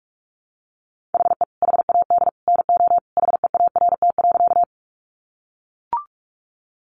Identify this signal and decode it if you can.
Morse code, heard as a keyed tone, about 44 wpm, 700 Hz. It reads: HE HAD NO HEART.
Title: Sentences from Top 1200 Words 45wpm